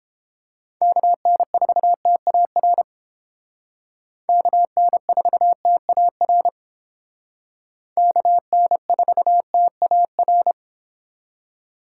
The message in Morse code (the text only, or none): KN4TAR KN4TAR KN4TAR